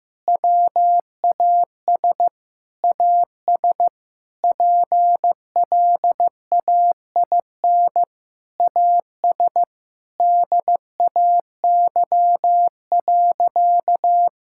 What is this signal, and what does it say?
Morse code, heard as a keyed tone, about 15 wpm, 700 Hz. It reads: WAS AS PLAIN AS DAY.
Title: Sentences from Top 1000 Words - Review Part 5 - 15wpm